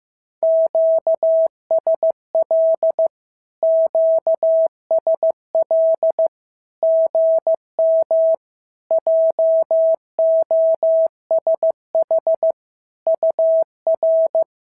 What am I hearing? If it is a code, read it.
Morse code, heard as a keyed tone, about 15 wpm, 650 Hz. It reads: QSL QSL GM JOSH UR
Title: SOTA QSO Practice 15wpm